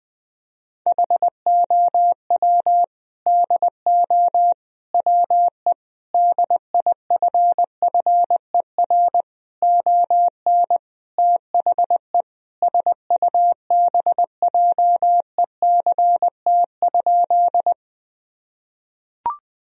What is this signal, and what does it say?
Morse code, heard as a keyed tone, about 21 wpm, 700 Hz. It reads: HOW DO WE DIFFER ON THE SUBJECT?
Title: Sentences from Top 1000 Words - Review Part 1 - 20wpm